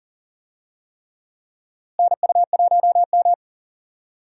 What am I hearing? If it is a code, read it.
Morse code, heard as a keyed tone, about 40 wpm, 700 Hz. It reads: NU1M
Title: USA Call Signs - 4 Characters - Repeated 3x - 40wpm